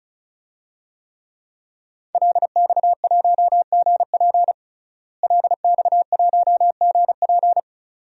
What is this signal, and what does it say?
Morse code, heard as a keyed tone, about 34 wpm, 700 Hz. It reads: LX1GP LX1GP
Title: DXCC-100 Call Signs - 5 Characters - Repeated 2x - 35wpm